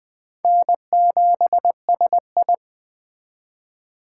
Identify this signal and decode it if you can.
Morse code, heard as a keyed tone, about 21 wpm, 700 Hz. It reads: N7SI
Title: Speed Racing - USA 1x2 Call Signs 17wpm